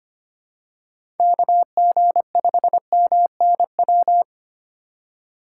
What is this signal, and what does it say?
Morse code, heard as a keyed tone, about 25 wpm, 700 Hz. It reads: KG5MNW
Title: USA Call Signs - 6 Characters - 4x Character Spacing - 25wpm